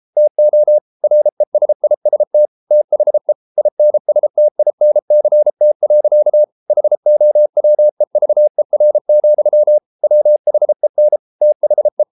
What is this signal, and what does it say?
Morse code, heard as a keyed tone, about 33 wpm, 600 Hz. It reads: TO RESIST THE INSTINCT. HOWEVER, WHEN THE